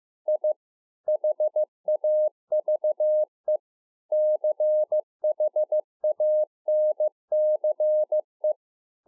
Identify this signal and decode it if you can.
Morse code, heard as a keyed tone, about 15 wpm, 610 Hz. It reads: I HAVE CHANCE